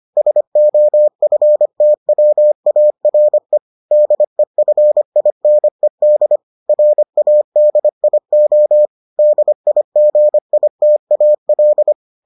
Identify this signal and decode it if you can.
Morse code, heard as a keyed tone, about 25 wpm, 600 Hz. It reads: SOFTWARE DEFINED RADIO DIGITAL